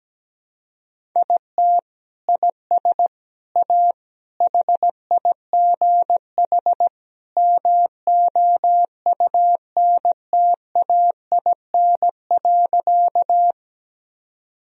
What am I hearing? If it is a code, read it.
Morse code, heard as a keyed tone, about 17 wpm, 700 Hz. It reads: IT IS A HIGH MOUNTAIN.
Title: Sentences from Top 1000 Words - Review Part 1 - 17wpm